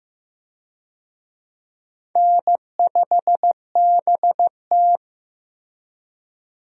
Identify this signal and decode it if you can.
Morse code, heard as a keyed tone, about 15 wpm, 700 Hz. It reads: N5BT
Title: USA Call Signs - 4 Characters - Repeated 2x - 15wpm